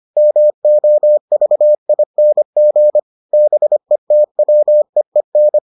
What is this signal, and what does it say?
Morse code, heard as a keyed tone, about 25 wpm, 600 Hz. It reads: MOVING BETWEEN